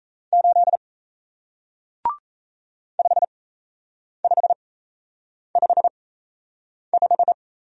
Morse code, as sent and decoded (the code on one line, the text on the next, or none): ---.. / ..... ..... ..... .....
8 5555